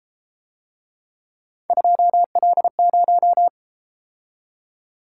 Morse code, heard as a keyed tone, about 31 wpm, 700 Hz. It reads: 2L0